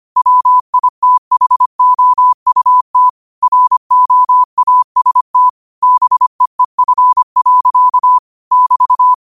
Morse code, heard as a keyed tone, about 25 wpm, 1000 Hz. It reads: WITHOUT ROAST BEEF. <BT>
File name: qotdfiles / 25wpm